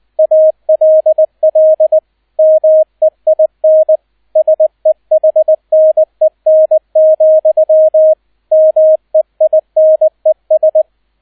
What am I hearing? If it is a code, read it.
Morse code, heard as a keyed tone, about 20 wpm, 630 Hz, with radio noise. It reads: ALL MEIN SEHNEN, MEINES
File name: Erster Streich 20wpm